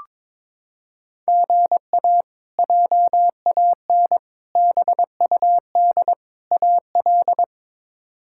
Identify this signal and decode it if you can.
Morse code, heard as a keyed tone, about 22 wpm, 700 Hz. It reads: GA JAN BUD AL
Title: K1USN SST QSO Practice - 22wpm